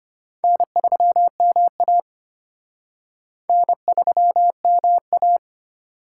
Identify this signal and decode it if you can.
Morse code, heard as a keyed tone, about 33 wpm, 700 Hz. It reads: N3MA N3MA